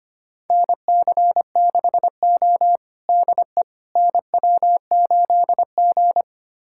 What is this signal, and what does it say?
Morse code, heard as a keyed tone, about 25 wpm, 700 Hz. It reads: NC6O DE NW8G
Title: USA Callsign DE Callsign - 4 Characters - 25wpm